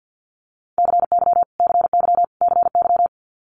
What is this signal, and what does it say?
Morse code, heard as a keyed tone, about 50 wpm, 700 Hz. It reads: CY CY CY
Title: Multi-letter Phonemes Repeated 3x 50wpm